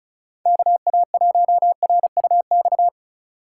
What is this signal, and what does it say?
Morse code, heard as a keyed tone, about 36 wpm, 700 Hz. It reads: KA1RUX
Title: USA Call Signs - 6 Characters - 4x Character Spacing - 35wpm